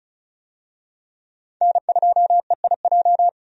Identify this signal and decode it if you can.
Morse code, heard as a keyed tone, about 34 wpm, 700 Hz. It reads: N2EIJ
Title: USA Call Signs - 5 Characters - 35wpm